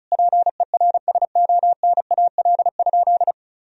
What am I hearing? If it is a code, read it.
Morse code, heard as a keyed tone, about 36 wpm, 700 Hz. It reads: PERSONAL?
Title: Sentences from Top 1000 Words - Review Part 4 - 35wpm